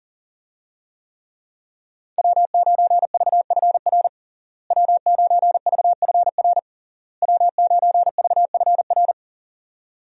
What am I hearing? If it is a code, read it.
Morse code, heard as a keyed tone, about 40 wpm, 700 Hz. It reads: W9VFR W9VFR W9VFR
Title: USA Call Signs - 5 Characters - Repeated 3x - 40wpm